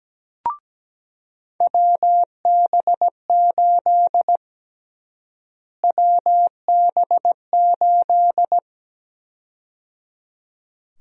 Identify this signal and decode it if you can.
Morse code, heard as a keyed tone, about 17 wpm, 700 Hz. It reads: WB8 WB8
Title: USA Call Sign Prefixes - Repeated 2x - 17wpm